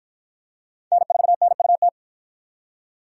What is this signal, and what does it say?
Morse code, heard as a keyed tone, about 47 wpm, 700 Hz. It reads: N4NUT